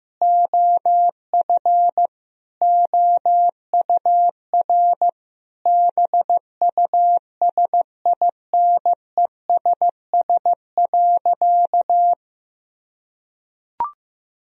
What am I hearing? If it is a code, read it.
Morse code, heard as a keyed tone, about 15 wpm, 700 Hz. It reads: OF OUR BUSINESS.
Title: Sentences from Top 1000 Words - Review Part 4 - 15wpm